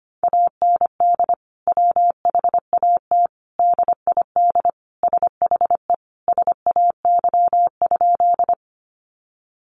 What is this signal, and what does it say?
Morse code, heard as a keyed tone, about 25 wpm, 700 Hz. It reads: AND WHAT DID SHE SAY?
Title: Sentences from Top 200 Words 25wpm